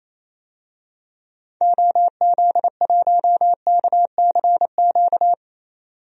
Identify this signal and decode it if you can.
Morse code, heard as a keyed tone, about 28 wpm, 700 Hz. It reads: OZ1KCQ
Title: DXCC-100 Call Signs - 6 Characters - 28wpm